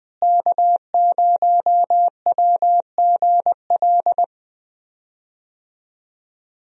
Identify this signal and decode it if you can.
Morse code, heard as a keyed tone, about 20 wpm, 700 Hz. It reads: K0WGL
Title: USA Call Signs - 5 Characters - Repeated 3x - 20wpm